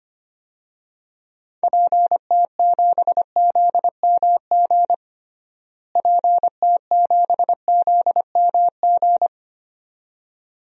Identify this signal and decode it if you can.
Morse code, heard as a keyed tone, about 25 wpm, 700 Hz. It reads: PT7ZMG PT7ZMG
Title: DXCC-100 Call Signs - 6 Characters - Repeated 2x - 25wpm